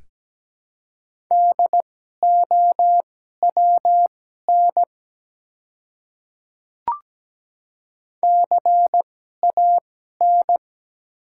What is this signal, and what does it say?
Morse code, heard as a keyed tone, about 17 wpm, spaced as for 12 wpm, 700 Hz. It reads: DOWN CAN
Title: 2x Character Spacing - Top 200 Words - 17wpm